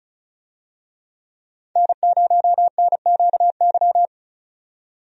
Morse code, heard as a keyed tone, about 34 wpm, 700 Hz. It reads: N0NQY
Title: USA Call Signs - 5 Characters - 35wpm